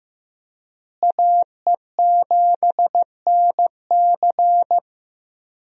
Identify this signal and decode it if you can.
Morse code, heard as a keyed tone, about 15 wpm, 700 Hz. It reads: AE7NC
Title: USA Call Signs - 5 Characters - Repeated 2x - 15wpm